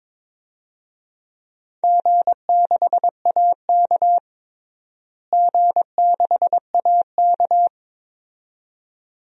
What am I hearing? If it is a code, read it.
Morse code, heard as a keyed tone, about 22 wpm, 700 Hz. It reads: G6AK G6AK
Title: DXCC-100 Call Signs - 4 Characters - Repeated 2x - 22wpm